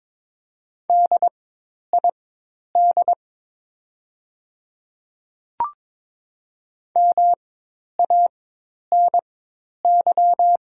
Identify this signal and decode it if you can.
Morse code, heard as a keyed tone, about 22 wpm, spaced as for 10 wpm, 700 Hz. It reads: DID MANY
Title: 4x Character Spacing - Top 200 Words - 22wpm